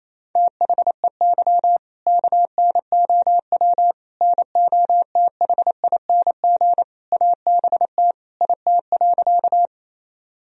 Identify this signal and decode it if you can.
Morse code, heard as a keyed tone, about 28 wpm, 700 Hz. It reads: THEY KNOW NOTHING ABT IT.
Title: Sentences from Top 500 Words and Abbreviations 28wpm